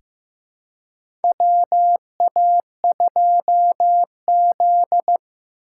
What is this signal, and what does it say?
Morse code, heard as a keyed tone, about 15 wpm, 700 Hz. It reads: WA2Z
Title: USA Call Signs - 4x Character Spacing - 15wpm